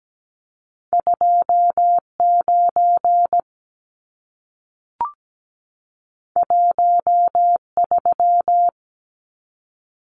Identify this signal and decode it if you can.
Morse code, heard as a keyed tone, about 17 wpm, 700 Hz. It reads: 29 13
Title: Numbers - 2-Digit 17wpm